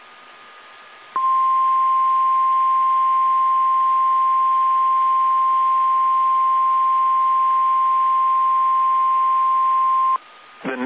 Emergency Alert System (SAME) tones (the 1050 Hz attention tone). It is heard from 1.1 to 10.2 s.